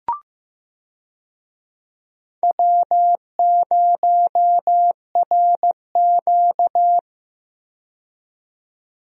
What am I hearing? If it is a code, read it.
Morse code, heard as a keyed tone, about 15 wpm, 700 Hz. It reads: W0RQ
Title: USA Call Signs - 4 Characters - 15wpm